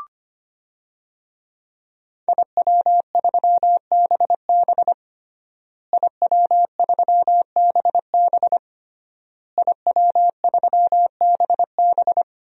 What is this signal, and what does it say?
Morse code, heard as a keyed tone, about 25 wpm, 700 Hz. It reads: IW3BB IW3BB IW3BB
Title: DXCC-100 Call Signs - 5 Characters - Repeated 3x - 25wpm